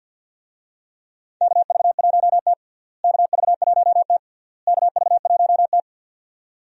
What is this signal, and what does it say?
Morse code, heard as a keyed tone, about 52 wpm, 700 Hz. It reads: KV1T KV1T KV1T
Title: USA Call Signs - 4 Characters - Repeated 3x - 50wpm